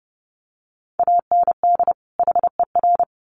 Morse code, heard as a keyed tone, about 30 wpm, 700 Hz. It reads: AND HER
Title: Sets of 2 Words 30wpm